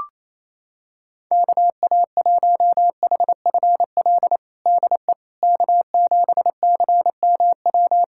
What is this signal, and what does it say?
Morse code, heard as a keyed tone, about 28 wpm, 700 Hz. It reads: KA1HFL DE K7CMW
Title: USA Callsign DE Callsign 28wpm